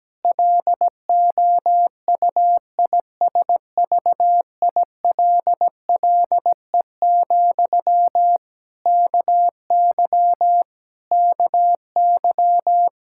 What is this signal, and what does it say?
Morse code, heard as a keyed tone, about 17 wpm, 700 Hz. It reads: LOUISVILLE, KY KY
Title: K1USN SST QSO Practice 17wpm